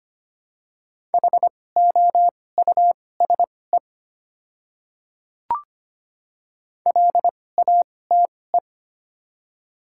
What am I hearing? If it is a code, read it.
Morse code, heard as a keyed tone, about 25 wpm, spaced as for 18 wpm, 700 Hz. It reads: HOUSE LATE
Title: 2x Character Spacing - Top 500 Words - 25wpm